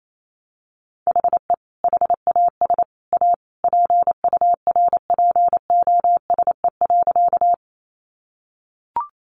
Morse code, heard as a keyed tone, about 28 wpm, 700 Hz. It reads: HE HAS A PURPOSE.
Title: Sentences from Top 800 Words 28wpm